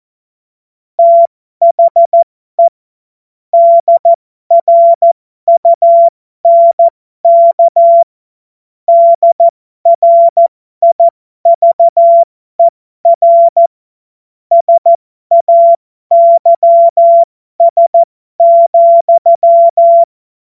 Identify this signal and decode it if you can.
Morse code, heard as a keyed tone, about 14 wpm, spaced as for 12 wpm, 680 Hz. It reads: THE DRUNK DRIVER SAYS,
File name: Morsekurs / Witze